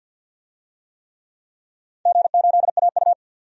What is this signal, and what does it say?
Morse code, heard as a keyed tone, about 52 wpm, 700 Hz. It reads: G8AU
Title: DXCC-100 Call Signs - 4 Characters - 50wpm